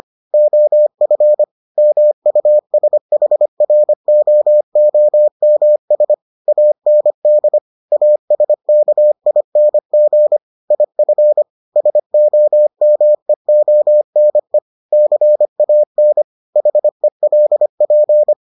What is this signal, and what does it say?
Morse code, heard as a keyed tone, about 25 wpm, 600 Hz. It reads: OF MUSHROOMS AND ASKING IF SOMEONE CAN HELP